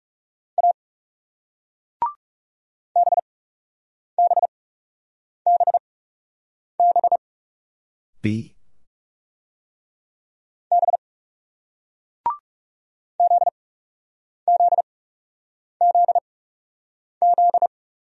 Morse code, heard as a keyed tone, about 37 wpm, spaced as for 8 wpm, 700 Hz. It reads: A BBBB B ZZZZ
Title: Speed-Racing Single Letters 30wpm